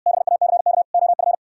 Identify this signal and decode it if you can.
Morse code, heard as a keyed tone, about 68 wpm, 700 Hz. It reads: BACK OF